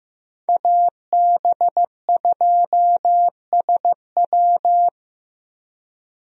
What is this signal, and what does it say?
Morse code, heard as a keyed tone, about 16 wpm, 700 Hz. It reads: AB2SW
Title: USA Call Signs - 5 Characters - 15wpm